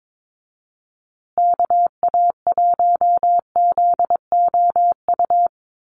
Morse code, heard as a keyed tone, about 22 wpm, 700 Hz. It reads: KA1ZOU
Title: USA Call Signs Repeated 2x 22wpm